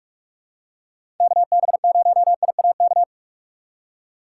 Morse code, heard as a keyed tone, about 45 wpm, 700 Hz. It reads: KB0IAK
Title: USA Call Signs - 6 Characters - 2x Character Spacing - 45wpm